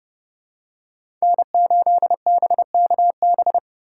Morse code, heard as a keyed tone, about 30 wpm, 700 Hz. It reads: N8BKB
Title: USA Call Signs - 5 Characters - 2x Character Spacing - 30wpm